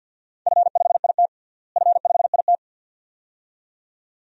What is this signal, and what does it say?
Morse code, heard as a keyed tone, about 46 wpm, 700 Hz. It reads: F5IT F5IT
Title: DXCC-100 Call Signs - 4 Characters - Repeated 2x - 50wpm